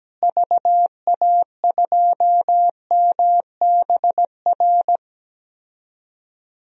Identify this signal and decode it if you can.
Morse code, heard as a keyed tone, about 17 wpm, 700 Hz. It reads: VA2MBR
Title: DXCC-100 Call Signs - 6 Characters - Repeated 3x - 17wpm